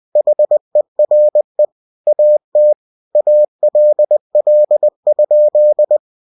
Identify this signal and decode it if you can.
Morse code, heard as a keyed tone, about 20 wpm, 600 Hz. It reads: HERE AT ALL?
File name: cw-books / winnie-the-pooh / winnie-20wpm-page-0013